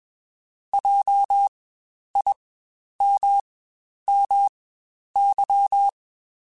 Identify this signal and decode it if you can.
Morse code, heard as a keyed tone, about 21 wpm, spaced as for 10 wpm, 800 Hz. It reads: JIMMY
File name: Spaced Alphas / Spaced Alphas mp3s / Js sp @20